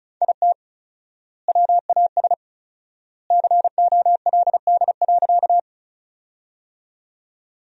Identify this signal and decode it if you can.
Morse code, heard as a keyed tone, about 36 wpm, 700 Hz. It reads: IT WAS COLD.
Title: 4x Word-Spacing - Sentences from Top 400 Words - 35wpm